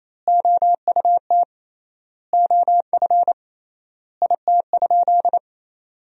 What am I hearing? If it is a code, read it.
Morse code, heard as a keyed tone, about 28 wpm, 700 Hz. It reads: OUT OF IT?
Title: 3x Word-Spacing - Sentences from Top 300 Words - 28wpm